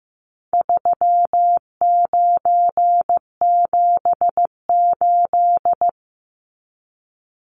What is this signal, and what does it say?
Morse code, heard as a keyed tone, about 15 wpm, 700 Hz. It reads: 3978
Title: Numbers - 4-Digit 15wpm